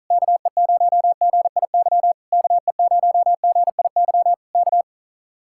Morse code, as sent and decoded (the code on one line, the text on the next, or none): -.- . ----- --. .. -.-- / -.- . ----- --. .. -.-- / -.-
KE0GIY KE0GIY K